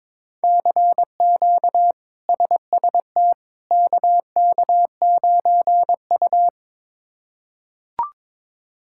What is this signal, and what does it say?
Morse code, heard as a keyed tone, about 22 wpm, 700 Hz. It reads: CQ SST KK9U
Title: K1USN SST QSO Practice - 22wpm